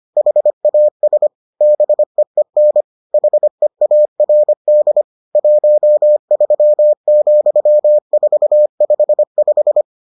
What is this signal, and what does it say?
Morse code, heard as a keyed tone, about 25 wpm, 600 Hz. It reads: HAS BEEN HEARD 13,455